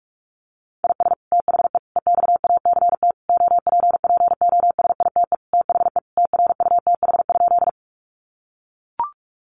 Morse code, heard as a keyed tone, about 44 wpm, 700 Hz. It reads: IS THE EXACT OPPOSITE THE TRUTH?
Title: Sentences from Top 1300 Words 45wpm